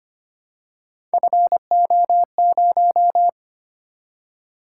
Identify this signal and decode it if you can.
Morse code, heard as a keyed tone, about 25 wpm, 700 Hz. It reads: FO0
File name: Three Letter-Number Combinations - 25wpm